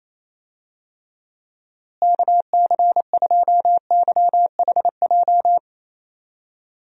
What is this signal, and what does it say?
Morse code, heard as a keyed tone, about 28 wpm, 700 Hz. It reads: KC2YHJ
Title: USA Call Signs - 6 Characters - Repeated 3x - 28wpm